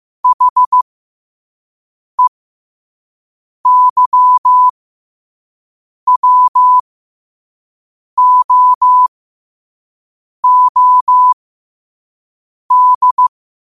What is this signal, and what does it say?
Morse code, heard as a keyed tone, about 15 wpm, spaced as for 5 wpm, 1000 Hz. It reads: HEYWOOD
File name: qotdfiles / 5wpm